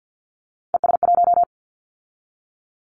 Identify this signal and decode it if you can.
Morse code, heard as a keyed tone, about 48 wpm, 700 Hz. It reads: ES1
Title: Callsign Prefixes for Top 100 DXCC - 50wpm